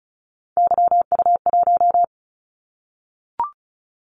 Morse code, heard as a keyed tone, about 36 wpm, 690 Hz. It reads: YU1
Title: Callsign Prefixes for Top 25 DXCC - 35wpm